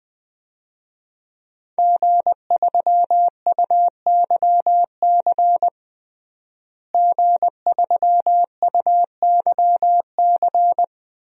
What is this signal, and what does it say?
Morse code, heard as a keyed tone, about 21 wpm, 700 Hz. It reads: G3UYC G3UYC
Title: DXCC-100 Call Signs - 5 Characters - Repeated 3x - 20wpm